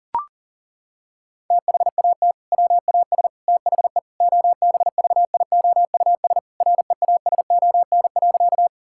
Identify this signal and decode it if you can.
Morse code, heard as a keyed tone, about 40 wpm, 700 Hz. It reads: THAT WAS THE OBVIOUS REASON.
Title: Sentences from Top 1600 Words 40wpm